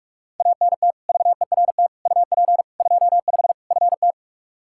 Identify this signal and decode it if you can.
Morse code, heard as a keyed tone, about 42 wpm, 700 Hz. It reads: ANT VERT UP 25 FT
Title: Practice QSO - Exchange 2 - Info - 45wpm